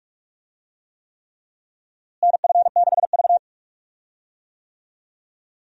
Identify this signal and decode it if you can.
Morse code, heard as a keyed tone, about 46 wpm, 700 Hz. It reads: NF6V
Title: USA Call Signs - 4 Characters - Repeated 3x - 45wpm